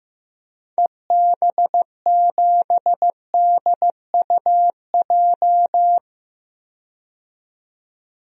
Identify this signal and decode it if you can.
Morse code, heard as a keyed tone, about 16 wpm, 700 Hz. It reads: EB7DUJ
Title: DXCC-100 Call Signs - 6 Characters - 15wpm